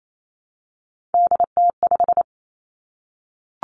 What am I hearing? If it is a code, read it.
Morse code, heard as a keyed tone, about 28 wpm, 700 Hz. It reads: DT5
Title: Three Challenging Letter-Number Combinations 28wpm